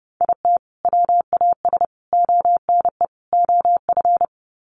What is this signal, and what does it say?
Morse code, heard as a keyed tone, about 30 wpm, 700 Hz. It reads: IT WAS ONE OF
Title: Sets of 4 Words 30wpm